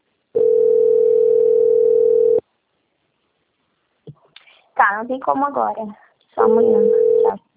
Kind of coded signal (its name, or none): telephone call progress tone